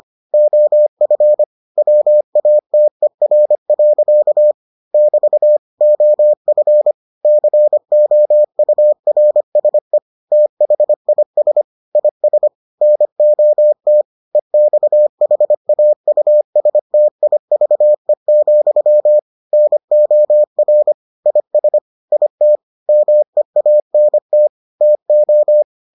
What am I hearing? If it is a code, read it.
Morse code, heard as a keyed tone, about 25 wpm, 600 Hz. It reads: OF WATER. <BT> OF COURSE THIS IS NOT EXHAUSTIVE, NOR IS IT MEANT TO